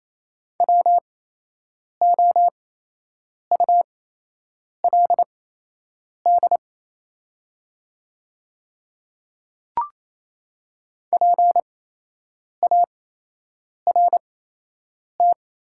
Morse code, heard as a keyed tone, about 28 wpm, spaced as for 8 wpm, 700 Hz. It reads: WOULD PART
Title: 8x Character Spacing - Top 300 Words - 28wpm